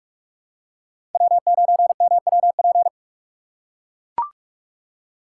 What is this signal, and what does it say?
Morse code, heard as a keyed tone, about 42 wpm, 700 Hz. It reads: W9MWP